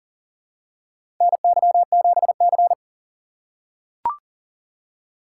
Morse code, heard as a keyed tone, about 40 wpm, 700 Hz. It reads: NY7C
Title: USA Call Signs - 4 Characters - 3x Character Spacing - 40wpm